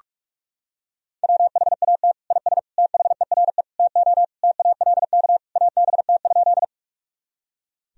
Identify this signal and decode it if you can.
Morse code, heard as a keyed tone, about 46 wpm, 700 Hz. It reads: WHAT IS THERE TO TALK ABT?
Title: Sentences from Top 400 Words and Abbreviations 45wpm